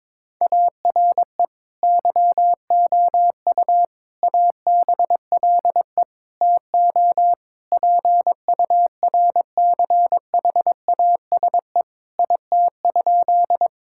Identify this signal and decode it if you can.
Morse code, heard as a keyed tone, about 22 wpm, 700 Hz. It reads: ARE YOU ABLE TO PURCHASE IT?
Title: Sentences from Top 1000 Words - Review Part 5 - 22wpm